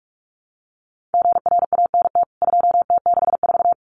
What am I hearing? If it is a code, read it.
Morse code, heard as a keyed tone, about 44 wpm, 700 Hz. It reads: GRANT 2T64
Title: CWT Exchange 45wpm